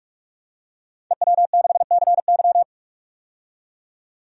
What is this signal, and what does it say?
Morse code, heard as a keyed tone, about 46 wpm, 700 Hz. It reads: EW6CY
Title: DXCC-100 Call Signs - 5 Characters - 45wpm